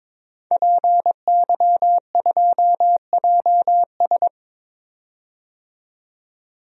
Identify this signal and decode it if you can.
Morse code, heard as a keyed tone, about 21 wpm, 700 Hz. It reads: PY2JS
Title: DXCC-100 Call Signs - 5 Characters - 22wpm